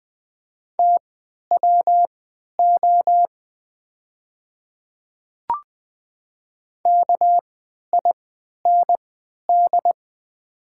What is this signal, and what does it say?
Morse code, heard as a keyed tone, about 20 wpm, spaced as for 11 wpm, 700 Hz. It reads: TWO KIND
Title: 3x Character Spacing - Top 300 Words - 20wpm